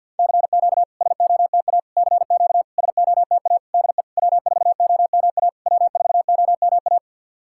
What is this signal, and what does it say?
Morse code, heard as a keyed tone, about 51 wpm, 690 Hz. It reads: CQ SOTA CQ SOTA DE W4OMA W4OMA
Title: SOTA QSO Practice 50wpm